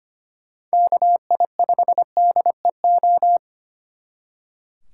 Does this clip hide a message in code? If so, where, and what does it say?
Yes: Morse code from 0.7 to 3.4 s, heard as a keyed tone, about 33 wpm, 700 Hz. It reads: KI5DEO